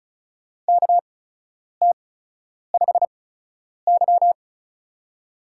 Morse code, heard as a keyed tone, about 36 wpm, spaced as for 10 wpm, 700 Hz. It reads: KT5Y
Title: USA Call Signs - 8x Character Spacing - 35wpm